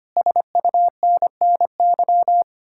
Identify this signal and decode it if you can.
Morse code, heard as a keyed tone, about 33 wpm, 700 Hz. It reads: SUNNY